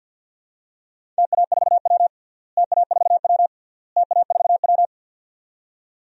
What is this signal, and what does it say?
Morse code, heard as a keyed tone, about 51 wpm, 700 Hz. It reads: TA4W TA4W TA4W
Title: DXCC-100 Call Signs - 4 Characters - Repeated 3x - 50wpm